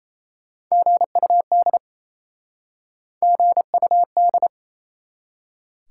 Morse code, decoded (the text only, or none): GUD GUD